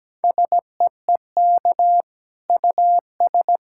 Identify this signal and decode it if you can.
Morse code, heard as a keyed tone, about 17 wpm, 700 Hz. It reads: SEEK US
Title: Sentences from Top 1600 Words 17wpm